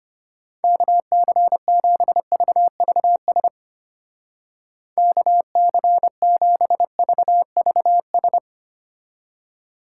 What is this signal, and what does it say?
Morse code, heard as a keyed tone, about 32 wpm, 700 Hz. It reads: KC7VVS KC7VVS